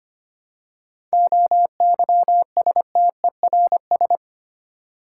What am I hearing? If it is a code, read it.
Morse code, heard as a keyed tone, about 25 wpm, 700 Hz. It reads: OYSTERS
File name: practice / 25wpm / Difficult Words with Hints - 25wpm